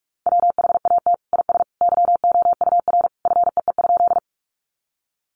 Morse code, heard as a keyed tone, about 44 wpm, 700 Hz. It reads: WHAT IS YOUR FEE?
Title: Sentences from Top 1500 Words 45wpm